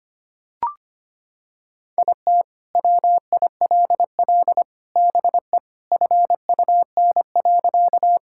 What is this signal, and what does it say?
Morse code, heard as a keyed tone, about 25 wpm, 700 Hz. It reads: IT WILL BE FUN.
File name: practice / 25wpm / Sentences with 3-Word Phrases - 25wpm